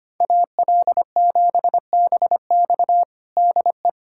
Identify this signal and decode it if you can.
Morse code, heard as a keyed tone, about 25 wpm, 700 Hz. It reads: AL7BX DE
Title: USA Callsign DE Callsign - 5 Characters - 25wpm